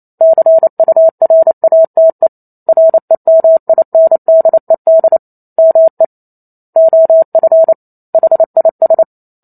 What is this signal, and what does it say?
Morse code, heard as a keyed tone, about 28 wpm, 650 Hz. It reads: CURATE REMINDED ME OF HIS